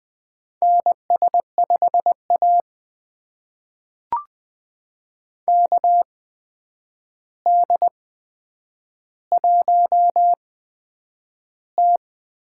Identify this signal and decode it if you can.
Morse code, heard as a keyed tone, about 20 wpm, spaced as for 9 wpm, 700 Hz. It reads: NS5A KD1T
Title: USA Call Signs - 4 Characters - 8x Character Spacing - 20wpm